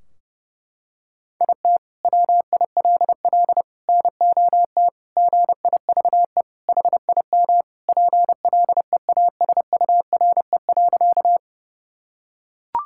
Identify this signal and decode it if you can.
Morse code, heard as a keyed tone, about 32 wpm, 700 Hz. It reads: IT WILL NOT GIVE HIM PLEASURE.
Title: Sentences from Top 1600 Words 30wpm